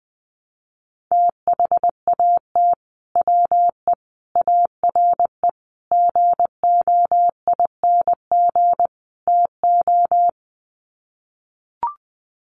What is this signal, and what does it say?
Morse code, heard as a keyed tone, about 20 wpm, 700 Hz. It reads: THAT WE ARE GOING TO
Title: Sets of 5 Words Encore 20wpm